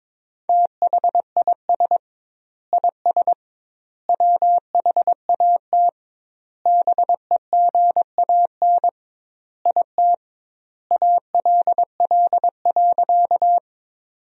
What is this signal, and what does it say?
Morse code, heard as a keyed tone, about 21 wpm, 700 Hz. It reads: THIS IS WHAT BEGAN IT ALL.
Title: 2x Word-Spacing - Sentences from Top 300 Words - 22wpm